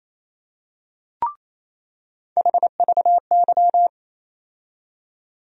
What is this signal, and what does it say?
Morse code, heard as a keyed tone, about 28 wpm, 700 Hz. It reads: HVY
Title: USA Call Sign Postfixes - 3 Letters 28wpm